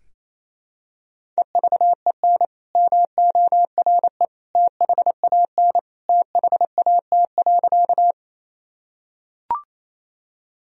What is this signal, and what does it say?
Morse code, heard as a keyed tone, about 28 wpm, 700 Hz. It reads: EVEN MORE THAN THAT.
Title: Sentences from Top 1000 Words - Review Part 1 - 28wpm